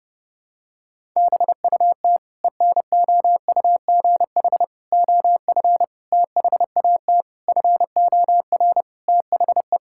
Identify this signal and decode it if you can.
Morse code, heard as a keyed tone, about 30 wpm, 700 Hz. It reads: BUT ENOUGH OF THAT FOR THE
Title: Sentences from Top 1000 Words - Review Part 5 - 30wpm